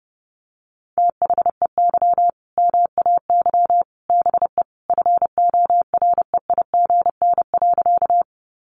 Morse code, heard as a keyed tone, about 30 wpm, 700 Hz. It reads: THEY MAY BE FOREIGN.
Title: Sentences from Top 1400 Words 30wpm